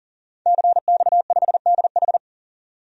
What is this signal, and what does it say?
Morse code, heard as a keyed tone, about 40 wpm, 700 Hz. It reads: CX5DH
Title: DXCC-100 Call Signs - 5 Characters - 40wpm